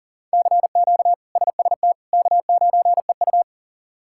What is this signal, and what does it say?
Morse code, heard as a keyed tone, about 40 wpm, 700 Hz. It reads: CQ SST K9EU
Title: K1USN SST QSO Practice - 40wpm